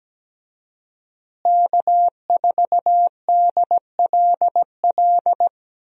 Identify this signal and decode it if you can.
Morse code, heard as a keyed tone, about 17 wpm, 700 Hz. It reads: K4DLL
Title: USA Call Signs - 5 Characters - Repeated 2x - 17wpm